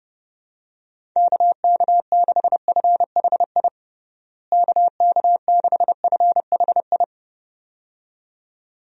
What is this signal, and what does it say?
Morse code, heard as a keyed tone, about 30 wpm, 700 Hz. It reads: KK6FHI KK6FHI
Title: USA Call Signs - 6 Characters - Repeated 2x - 30wpm